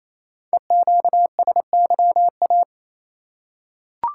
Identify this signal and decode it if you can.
Morse code, heard as a keyed tone, about 27 wpm, 700 Hz. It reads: EQSYA
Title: Code Groups - 5 Letters - 28wpm